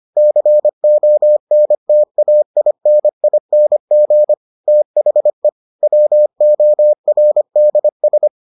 Morse code, heard as a keyed tone, about 25 wpm, 600 Hz. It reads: CONTAINING THE WORDS